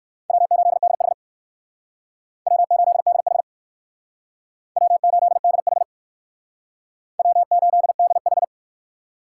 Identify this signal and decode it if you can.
Morse code, heard as a keyed tone, about 69 wpm, 700 Hz. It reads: W8DH W8DH W8DH W8DH